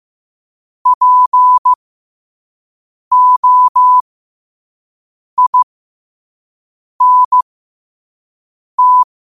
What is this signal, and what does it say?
Morse code, heard as a keyed tone, about 15 wpm, spaced as for 5 wpm, 1000 Hz. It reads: POINT
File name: qotdfiles / 5wpm